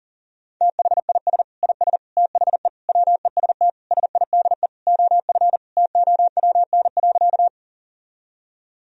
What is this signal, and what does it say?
Morse code, heard as a keyed tone, about 40 wpm, 700 Hz. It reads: THIS IS THE WEST SIDE OF TOWN.
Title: Sentences from Top 1000 Words - Review Part 2 - 40wpm